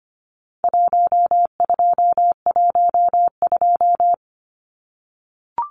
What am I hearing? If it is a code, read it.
Morse code, heard as a keyed tone, about 25 wpm, 700 Hz. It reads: 1212
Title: Numbers - Contest Style - 4-Digit 25wpm